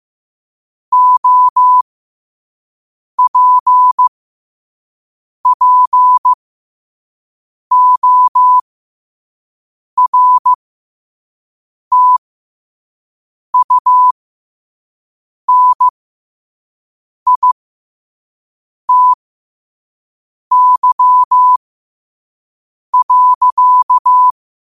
Morse code, heard as a keyed tone, about 15 wpm, spaced as for 5 wpm, 1000 Hz. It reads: OPPORTUNITY.
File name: 5wpm